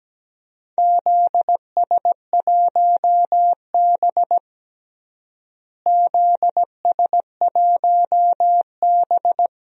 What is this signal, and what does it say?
Morse code, heard as a keyed tone, about 17 wpm, 700 Hz. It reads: ZS1B ZS1B
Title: DXCC-100 Call Signs - 4 Characters - Repeated 2x - 17wpm